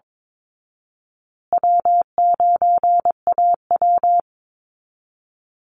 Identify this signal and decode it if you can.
Morse code, heard as a keyed tone, about 22 wpm, 700 Hz. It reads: W9AW
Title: Easy Call Signs Repeated 3x - 22wpm